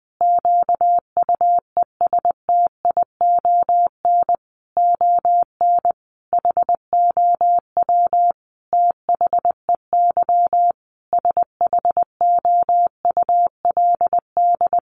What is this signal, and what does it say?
Morse code, heard as a keyed tone, about 20 wpm, 700 Hz. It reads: QUESTION ON HOW THEY SHOULD
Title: Sentences from Top 1300 Words 20wpm